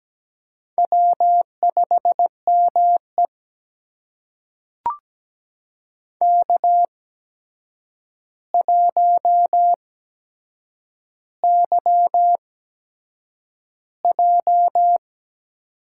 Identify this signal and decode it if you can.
Morse code, heard as a keyed tone, about 17 wpm, spaced as for 7 wpm, 700 Hz. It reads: W5ME K1YJ
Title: USA Call Signs - 4 Characters - 8x Character Spacing - 17wpm